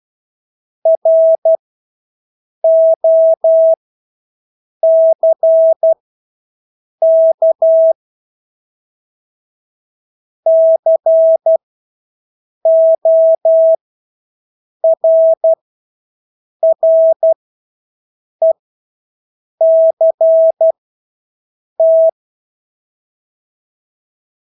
Morse code, heard as a keyed tone, about 12 wpm, spaced as for 6 wpm, 650 Hz. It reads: ROCK CORRECT
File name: Engelske ord 12-6 wpm